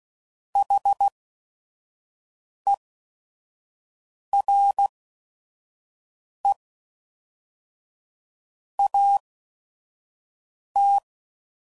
Morse code, heard as a keyed tone, about 16 wpm, spaced as for 5 wpm, 800 Hz. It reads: HERE AT
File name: CW Code Groups @5WPM / QSOs / WD0FBT 1